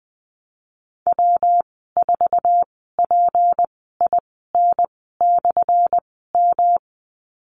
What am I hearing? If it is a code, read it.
Morse code, heard as a keyed tone, about 20 wpm, spaced as for 15 wpm, 700 Hz. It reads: W4PIN/M
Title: 179 Lesson 29g - Slash - Call Signs 20wpm